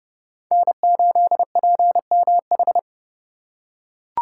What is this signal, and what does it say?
Morse code, heard as a keyed tone, about 29 wpm, 700 Hz. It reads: N8PMH